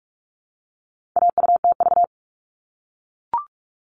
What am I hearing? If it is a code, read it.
Morse code, heard as a keyed tone, about 44 wpm, 700 Hz. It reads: AUTV
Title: All Cut Numbers - 4-Digit 45wpm